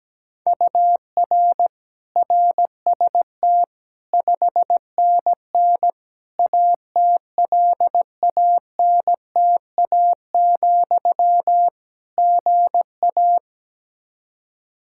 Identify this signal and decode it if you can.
Morse code, heard as a keyed tone, about 17 wpm, 700 Hz. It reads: UR RST 5NN ATLANTA, GA
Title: Practice QSO - Exchange 1 - Beginning - 17wpm